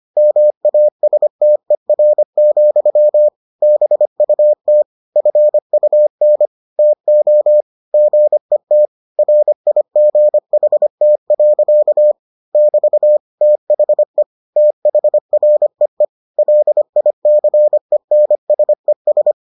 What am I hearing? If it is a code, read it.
Morse code, heard as a keyed tone, about 25 wpm, 600 Hz. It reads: MASTER, BUT FUN TO GET RIGHT. <BT> THE THREE LICENSES